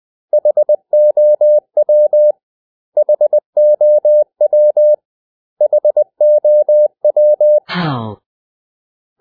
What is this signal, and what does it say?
Morse code, heard as a keyed tone, about 20 wpm, 600 Hz. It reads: HOW HOW HOW
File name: cw / 100 Common English Words